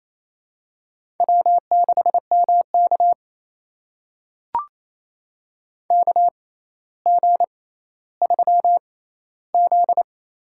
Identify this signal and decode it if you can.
Morse code, heard as a keyed tone, about 28 wpm, spaced as for 14 wpm, 700 Hz. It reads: W6MK KG3Z
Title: USA Call Signs - 4 Characters - 6x Character Spacing - 28wpm